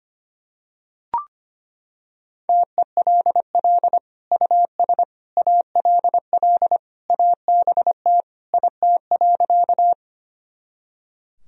Morse code, heard as a keyed tone, about 25 wpm, 700 Hz. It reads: TELL US ALL ABT IT.
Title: Sentences from Top 200 Words and Abbreviations 25wpm